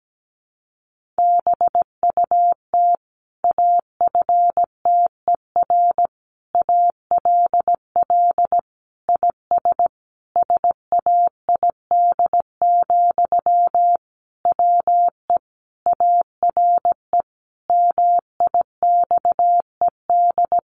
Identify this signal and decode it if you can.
Morse code, heard as a keyed tone, about 17 wpm, 700 Hz. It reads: BUT AFTER ALL IS SAID, WE ARE MIXED
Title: Sentences from Top 1500 Words 17wpm